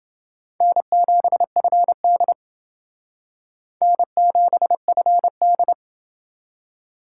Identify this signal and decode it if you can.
Morse code, heard as a keyed tone, about 29 wpm, 700 Hz. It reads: N7FD N7FD